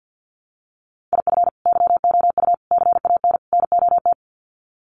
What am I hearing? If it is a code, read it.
Morse code, heard as a keyed tone, about 50 wpm, 700 Hz. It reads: IF YOU CAN NOT
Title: Sets of 4 Words 50wpm